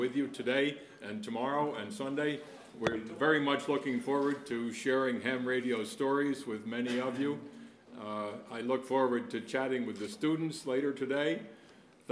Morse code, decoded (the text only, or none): none